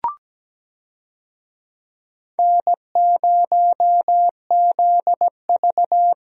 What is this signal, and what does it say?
Morse code, heard as a keyed tone, about 17 wpm, 700 Hz. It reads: N0ZV
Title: USA Call Signs - 4 Characters - Repeated 2x - 17wpm